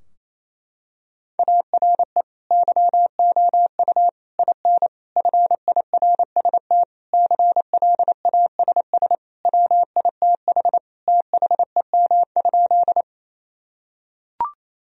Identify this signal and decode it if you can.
Morse code, heard as a keyed tone, about 28 wpm, 700 Hz. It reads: ARE YOU IN FIRST CLASS WITH THEM?
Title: Sentences from Top 1000 Words - Review Part 2 - 28wpm